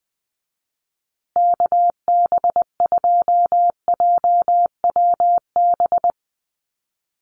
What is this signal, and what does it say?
Morse code, heard as a keyed tone, about 20 wpm, 700 Hz. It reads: KB2JWB
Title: USA Call Signs Repeated 3x 20wpm